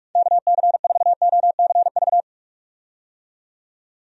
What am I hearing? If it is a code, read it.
Morse code, heard as a keyed tone, about 46 wpm, 700 Hz. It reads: KC4OCV
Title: USA Call Signs - 6 Characters - 45wpm